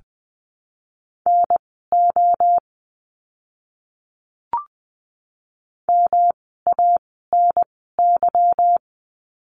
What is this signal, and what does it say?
Morse code, heard as a keyed tone, about 20 wpm, spaced as for 14 wpm, 700 Hz. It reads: NO MANY
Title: 230 Lesson 37f - Letter x - Words 20wpm